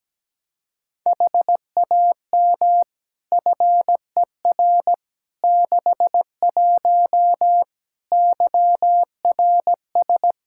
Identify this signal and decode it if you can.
Morse code, heard as a keyed tone, about 17 wpm, 700 Hz. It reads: HAM FER 61 YRS
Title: K1USN SST QSO Practice 17wpm